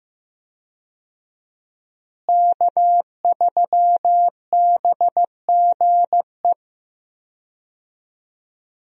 Morse code, heard as a keyed tone, about 15 wpm, 700 Hz. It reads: K3BGE
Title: USA Call Signs - 5 Characters - Repeated 2x - 15wpm